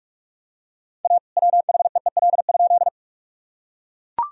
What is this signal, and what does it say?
Morse code, heard as a keyed tone, about 43 wpm, 700 Hz. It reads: A WHEEL?
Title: Sentences from Top 1000 Words - Review Part 3 - 45wpm